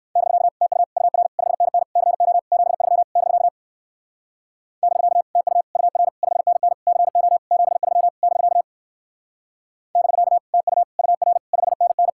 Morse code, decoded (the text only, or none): <BK> TU UR 5NN CO 73 <BK> <BK> TU UR 5NN CO 73 <BK> <BK> TU UR 5NN